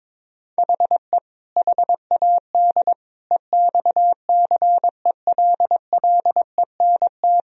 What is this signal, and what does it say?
Morse code, heard as a keyed tone, about 21 wpm, 700 Hz. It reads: HE HAD EXCELLENT
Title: Sentences from Top 1000 Words - Review Part 3 - 22wpm